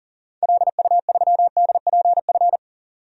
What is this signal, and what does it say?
Morse code, heard as a keyed tone, about 40 wpm, 700 Hz. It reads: LU3DPF
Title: DXCC-100 Call Signs - 6 Characters - 40wpm